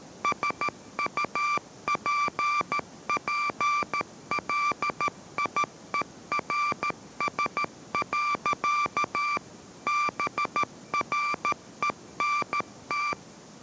Morse code, heard as a keyed tone, about 13 wpm, 1170 Hz, with noise. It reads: SUPPLIERS. BRENT